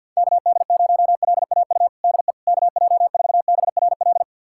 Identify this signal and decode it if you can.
Morse code, heard as a keyed tone, about 50 wpm, 700 Hz. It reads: KD0LAU DE KJ4BRL
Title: USA Callsign DE Callsign 50wpm